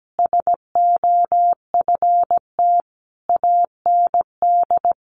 Morse code, heard as a keyed tone, about 17 wpm, 700 Hz. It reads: SOFT AND
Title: Sentences from Top 1500 Words 17wpm